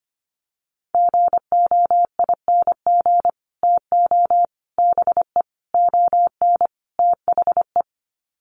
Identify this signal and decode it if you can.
Morse code, heard as a keyed tone, about 25 wpm, 700 Hz. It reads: GOING TO BE ON THE
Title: Sets of 5 Words Encore2 25wpm